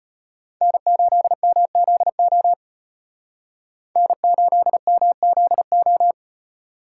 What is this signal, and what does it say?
Morse code, heard as a keyed tone, about 35 wpm, 700 Hz. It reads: N8MZO N8MZO